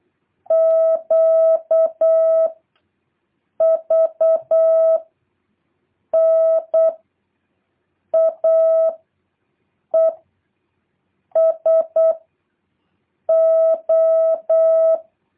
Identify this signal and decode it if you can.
Morse code, heard as a keyed tone, about 8 wpm, 650 Hz. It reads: Q V N A E S O
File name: jh8cbh / 2023cw